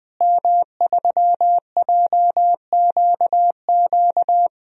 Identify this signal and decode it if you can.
Morse code, heard as a keyed tone, about 21 wpm, 700 Hz. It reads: M3JQQ
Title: DXCC-100 Call Signs - 5 Characters - Repeated 3x - 20wpm